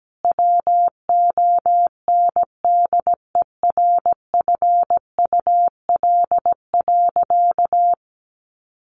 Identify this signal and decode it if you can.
Morse code, heard as a keyed tone, about 17 wpm, 700 Hz. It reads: WONDERFUL.
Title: Sentences from Top 1100 Words 17wpm